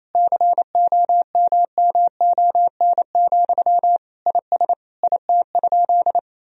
Morse code, heard as a keyed tone, about 28 wpm, 700 Hz. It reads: COMMON, IS IT?
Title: Sentences from Top 1000 Words - Review Part 3 - 28wpm